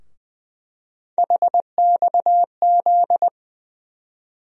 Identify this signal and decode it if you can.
Morse code, heard as a keyed tone, about 20 wpm, 700 Hz. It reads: HXZ
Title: USA Call Sign Postfixes - 3 Letters 20wpm